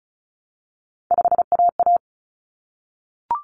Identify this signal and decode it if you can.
Morse code, heard as a keyed tone, about 34 wpm, 700 Hz. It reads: 5AA
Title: Common Cut Numbers - 3-Digit 35wpm